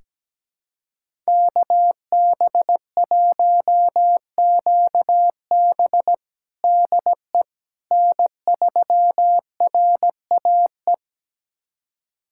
Morse code, heard as a keyed tone, about 17 wpm, 700 Hz. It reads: KB1QB DE N3RAE
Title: USA Callsign DE Callsign - 5 Characters - 17wpm